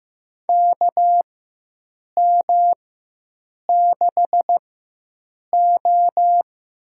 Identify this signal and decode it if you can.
Morse code, heard as a keyed tone, about 15 wpm, spaced as for 7 wpm, 700 Hz. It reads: KM6O
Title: USA Call Signs - 4x Character Spacing - 15wpm